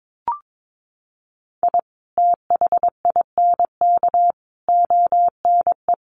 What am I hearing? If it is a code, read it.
Morse code, heard as a keyed tone, about 22 wpm, 700 Hz. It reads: I THINK ONE
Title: Sets of 5 Words 22wpm